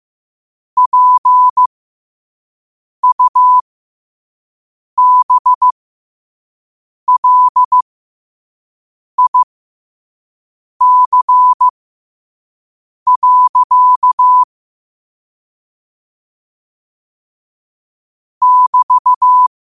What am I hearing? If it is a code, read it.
Morse code, heard as a keyed tone, about 15 wpm, spaced as for 5 wpm, 1000 Hz. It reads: PUBLIC. <BT>
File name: qotdfiles / 5wpm